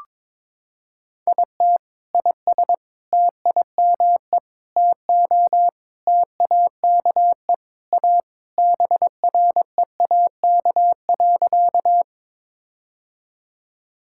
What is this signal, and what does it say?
Morse code, heard as a keyed tone, about 22 wpm, 700 Hz. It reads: IT IS TIME TO TAKE A BREAK.
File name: practice / 22wpm / Sentences with 4-Word Phrases - 22wpm